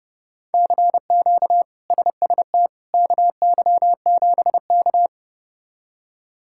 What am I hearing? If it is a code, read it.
Morse code, heard as a keyed tone, about 30 wpm, 700 Hz. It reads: CQ SST KY7K
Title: K1USN SST QSO Practice - 30wpm